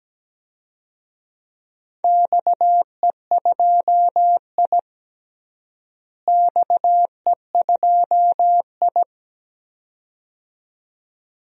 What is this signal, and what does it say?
Morse code, heard as a keyed tone, about 17 wpm, 700 Hz. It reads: XE2I XE2I
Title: DXCC-100 Call Signs - 4 Characters - Repeated 2x - 17wpm